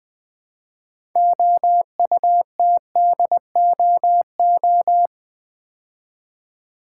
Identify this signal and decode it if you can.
Morse code, heard as a keyed tone, about 20 wpm, 700 Hz. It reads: OUTDOO
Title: Build A Word - 3 Letters At A Time - 8-Letter Words - 20wpm